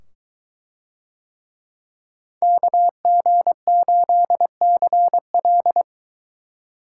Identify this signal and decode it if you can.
Morse code, heard as a keyed tone, about 22 wpm, 700 Hz. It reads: KG8CL